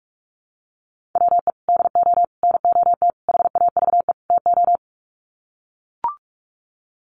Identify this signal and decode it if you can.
Morse code, heard as a keyed tone, about 44 wpm, 700 Hz. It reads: WE DO NOT HAVE TO
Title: Sets of 5 Words 45wpm